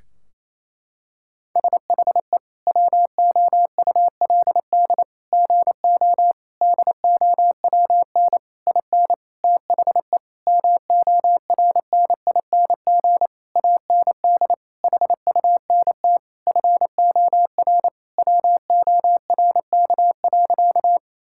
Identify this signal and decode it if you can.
Morse code, heard as a keyed tone, about 28 wpm, 700 Hz. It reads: SHE WOULD GO DOWN IN THE MORNING AND HUNT FOR WORK.
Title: Sentences from Top 1600 Words 28wpm